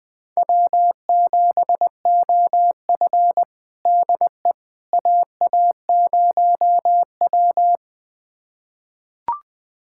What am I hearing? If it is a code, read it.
Morse code, heard as a keyed tone, about 20 wpm, 700 Hz. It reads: W7OF DE AA0W
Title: USA Callsign DE Callsign - 4 Characters - 20wpm